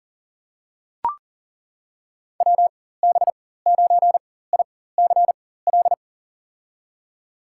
Morse code, heard as a keyed tone, about 39 wpm, spaced as for 18 wpm, 700 Hz. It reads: WB9ICL
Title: USA Call Signs - 6 Characters - 4x Character Spacing - 40wpm